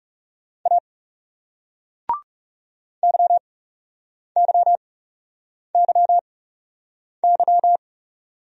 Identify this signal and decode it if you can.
Morse code, heard as a keyed tone, about 40 wpm, spaced as for 8 wpm, 700 Hz. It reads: A YYYY